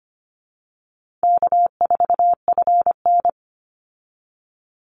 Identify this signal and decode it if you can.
Morse code, heard as a keyed tone, about 25 wpm, 700 Hz. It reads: K4FN
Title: Call Signs Repeated 2x - 25wpm